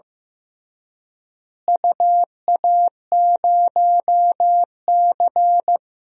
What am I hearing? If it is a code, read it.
Morse code, heard as a keyed tone, about 15 wpm, 700 Hz. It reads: UA0C
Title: DXCC-100 Call Signs - 4 Characters - Repeated 3x - 15wpm